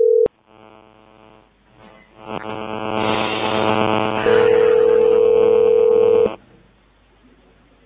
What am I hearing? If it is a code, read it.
Ringback tone (North America), 440 + 480 Hz.